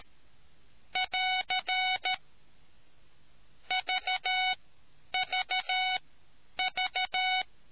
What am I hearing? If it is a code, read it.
Morse code, heard as a keyed tone, about 13 wpm, spaced as for 9 wpm, 740 Hz. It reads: <AR> VVV